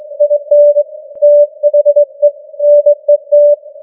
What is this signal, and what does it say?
Morse code, heard as a keyed tone, about 22 wpm, 590 Hz. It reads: IN THE NET
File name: challenge-178